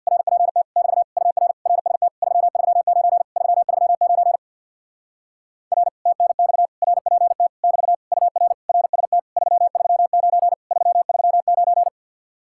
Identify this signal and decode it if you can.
Morse code, heard as a keyed tone, about 61 wpm, 700 Hz. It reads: RPT <BT> UR RST 339 339 R TNX RPT <BT> UR RST 339 339
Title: Speed-Racing Common QSO Phrases Encore - 50wpm